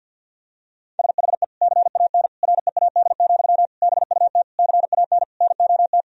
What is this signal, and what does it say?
Morse code, heard as a keyed tone, about 48 wpm, 700 Hz. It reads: SHE CAN READ, BUT CAN NOT
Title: Sentences from Top 1000 Words - Review Part 1 - 50wpm